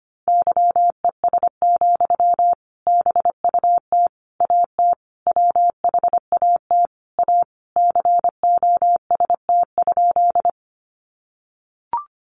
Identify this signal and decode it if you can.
Morse code, heard as a keyed tone, about 25 wpm, 700 Hz. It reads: YES, BUT AT WHAT A COST?
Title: Sentences from Top 800 Words 25wpm